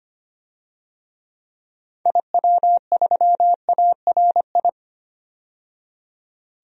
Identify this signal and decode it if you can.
Morse code, heard as a keyed tone, about 25 wpm, 700 Hz. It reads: IW3ARI
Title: DXCC-100 Call Signs - 6 Characters - 25wpm